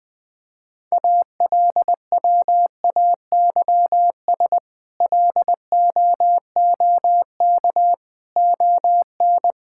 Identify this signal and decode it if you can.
Morse code, heard as a keyed tone, about 20 wpm, 700 Hz. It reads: ALWAYS LOOK ON
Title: Sentences with 8-Word Phrases - 20wpm